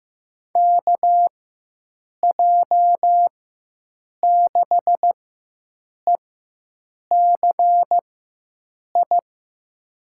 Morse code, heard as a keyed tone, about 16 wpm, spaced as for 7 wpm, 700 Hz. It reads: KJ6ECI
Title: USA Call Signs - 6 Characters - 4x Character Spacing - 15wpm